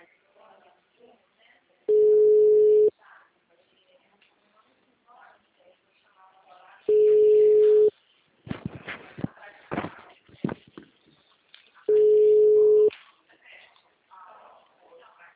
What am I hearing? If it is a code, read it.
Ringback tone (Europe, 425 Hz), with speech.
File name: out- -2032- - - .17092